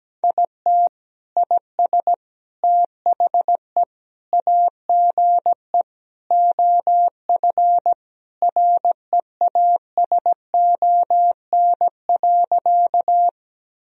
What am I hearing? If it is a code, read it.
Morse code, heard as a keyed tone, about 17 wpm, 700 Hz. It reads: IT IS THE AGE OF REASON.
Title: Sentences from Top 1000 Words - Review Part 3 - 17wpm